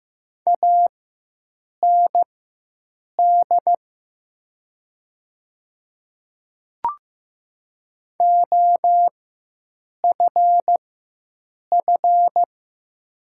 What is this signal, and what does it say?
Morse code, heard as a keyed tone, about 15 wpm, spaced as for 7 wpm, 690 Hz. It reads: AND OFF
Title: 4x Character Spacing - Top 200 Words - 15wpm